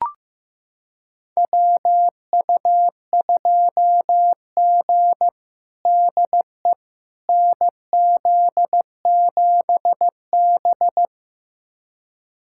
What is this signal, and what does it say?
Morse code, heard as a keyed tone, about 15 wpm, 700 Hz. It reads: WU2G DE NZ7B
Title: USA Callsign DE Callsign - 4 Characters - 15wpm